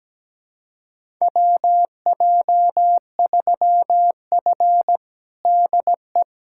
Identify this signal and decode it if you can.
Morse code, heard as a keyed tone, about 17 wpm, 700 Hz. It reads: WJ3F DE
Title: USA Callsign DE Callsign - 4 Characters - 17wpm